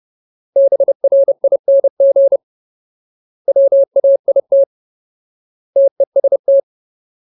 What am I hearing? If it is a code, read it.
Morse code, heard as a keyed tone, about 30 wpm, 550 Hz. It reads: BRING WAIT TEST
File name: Ultimate Word List / 30WPM Ultimate